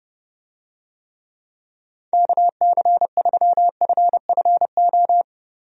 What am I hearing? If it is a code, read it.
Morse code, heard as a keyed tone, about 32 wpm, 700 Hz. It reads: KC3FFO
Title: USA Call Signs - 6 Characters - Repeated 3x - 30wpm